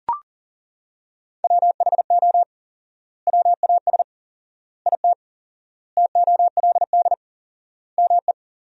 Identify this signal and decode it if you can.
Morse code, heard as a keyed tone, about 37 wpm, 700 Hz. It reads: WHO WAS IT TOLD ME
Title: 4x Word-Spacing - Sentences from Top 400 Words - 40wpm